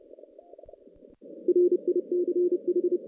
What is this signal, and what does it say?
Morse code, heard as a keyed tone, about 30 wpm, 410 Hz. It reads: RICH
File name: 2016 JAN CW NAQP / 2016 Jan naqp k5ot - 0-9